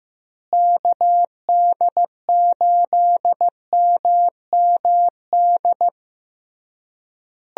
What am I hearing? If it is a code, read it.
Morse code, heard as a keyed tone, about 15 wpm, 700 Hz. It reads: KD8MMD
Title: USA Call Signs - 6 Characters - Repeated 3x - 15wpm